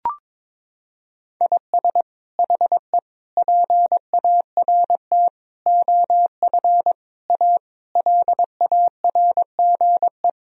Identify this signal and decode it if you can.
Morse code, heard as a keyed tone, about 22 wpm, 700 Hz. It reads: IS HE PART OF A LARGE
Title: Sentences from Top 1000 Words - Review Part 4 - 22wpm